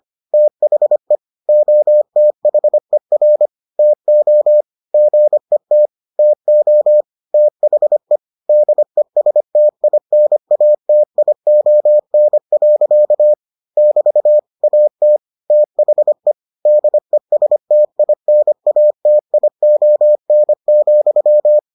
Morse code, heard as a keyed tone, about 25 wpm, 600 Hz. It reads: THE OTHER TO GET TO THE DESTINATION. <BT> AT THE DESTINATION,